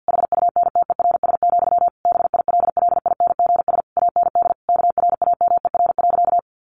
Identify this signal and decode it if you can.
Morse code, heard as a keyed tone, about 50 wpm, 700 Hz. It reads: HUNTERS, BILLINGS AND KRAMER.
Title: A Sound Of Thunder 50wpm